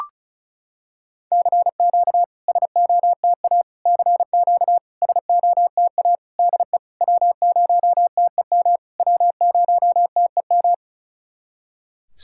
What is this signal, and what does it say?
Morse code, heard as a keyed tone, about 36 wpm, 690 Hz. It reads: CQ SOTA CQ SOTA DE W0TEM W0TEM
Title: SOTA QSO Practice 35wpm